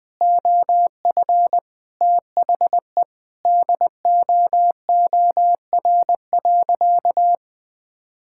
Morse code, heard as a keyed tone, about 21 wpm, 700 Hz. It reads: OF THE DOOR.
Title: Sentences with 5-Word Phrases - 20wpm